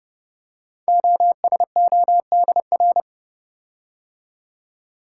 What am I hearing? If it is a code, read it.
Morse code, heard as a keyed tone, about 30 wpm, 700 Hz. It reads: OSODR
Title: Code Groups - 5 Letters - 30wpm